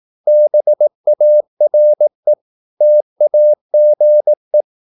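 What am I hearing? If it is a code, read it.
Morse code, heard as a keyed tone, about 18 wpm, 600 Hz. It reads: BARE TAGE
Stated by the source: Slow (QRS) Morse practice at 18 wpm, CW med Gnister.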